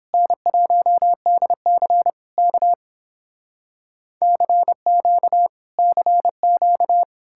Morse code, heard as a keyed tone, about 29 wpm, 700 Hz. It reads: N1DC K CQ CQ